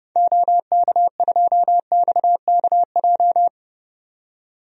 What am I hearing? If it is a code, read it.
Morse code, heard as a keyed tone, about 30 wpm, 700 Hz. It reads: OK2XKJ
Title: DXCC-100 Call Signs - 6 Characters - 30wpm